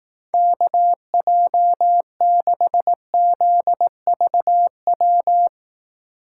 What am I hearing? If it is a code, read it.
Morse code, heard as a keyed tone, about 18 wpm, 700 Hz. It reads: KJ6ZVW
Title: Speed Racing - USA Call Signs 15wpm